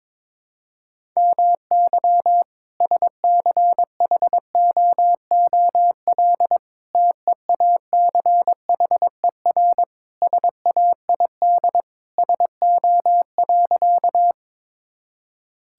Morse code, heard as a keyed tone, about 22 wpm, 700 Hz. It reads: MY SCHOOL TEACHER SAID SO.
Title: Sentences from Top 1000 Words - Review Part 6 - 22wpm